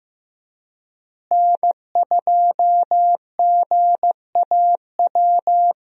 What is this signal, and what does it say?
Morse code, heard as a keyed tone, about 14 wpm, 700 Hz. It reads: N2GAW
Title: USA Call Signs - 5 Characters - Repeated 3x - 15wpm